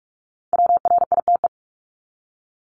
Morse code, heard as a keyed tone, about 44 wpm, 700 Hz. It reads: WRITE
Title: ING Suffix - Top 100 Words with Base Words 45wpm